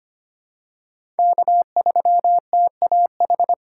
Morse code, heard as a keyed tone, about 25 wpm, 700 Hz. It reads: K3TAH
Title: USA Call Signs - 2x Character Spacing - 25wpm